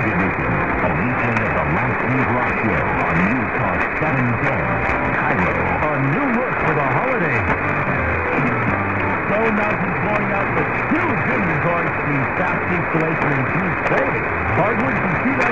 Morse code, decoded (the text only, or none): none